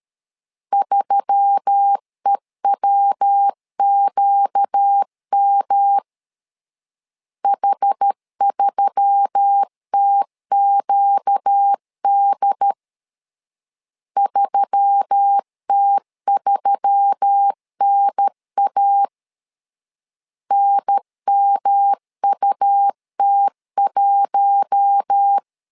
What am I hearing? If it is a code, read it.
Morse code, heard as a keyed tone, about 13 wpm, 800 Hz. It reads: 3EWQM H3TQD 3T3NA NMUT1
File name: LEZIONI / 4 Q1S3 12wpm 2023